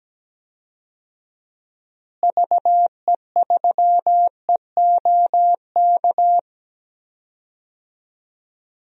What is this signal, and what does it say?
Morse code, heard as a keyed tone, about 17 wpm, 700 Hz. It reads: VE3EOK
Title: DXCC-100 Call Signs - 6 Characters - Repeated 3x - 17wpm